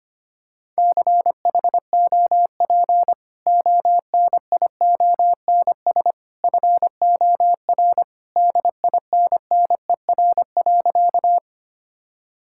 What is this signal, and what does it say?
Morse code, heard as a keyed tone, about 25 wpm, 700 Hz. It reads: CHOP ONIONS FOR DINNER.
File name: Easy Sentences with Hints - 25wpm